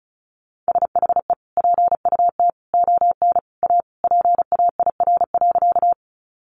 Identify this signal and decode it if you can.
Morse code, heard as a keyed tone, about 34 wpm, 700 Hz. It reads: SHE PUT ON A PAIR.
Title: Sentences from Top 1200 Words 35wpm